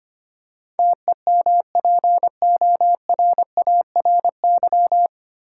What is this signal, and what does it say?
Morse code, heard as a keyed tone, about 25 wpm, 700 Hz. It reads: TEMPORARY
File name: Nine Letter Words - 25wpm